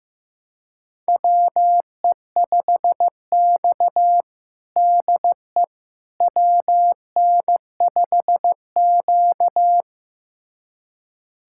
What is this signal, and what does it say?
Morse code, heard as a keyed tone, about 16 wpm, 700 Hz. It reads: WE5X DE WN5Q
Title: USA Callsign DE Callsign - 4 Characters - 15wpm